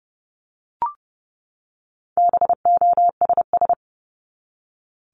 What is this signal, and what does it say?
Morse code, heard as a keyed tone, about 30 wpm, 700 Hz. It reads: BOSS
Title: Top 501-750 Words 30wpm